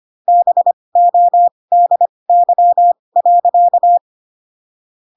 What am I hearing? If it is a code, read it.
Morse code, heard as a keyed tone, about 25 wpm, spaced as for 20 wpm, 700 Hz. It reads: BODY.
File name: Pinocchio / Pinocchio ch6 15 25 ik1hge